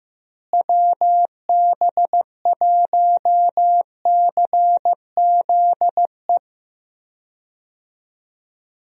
Morse code, heard as a keyed tone, about 15 wpm, 700 Hz. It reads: WB1CZE
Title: USA Call Signs - 6 Characters - Repeated 3x - 15wpm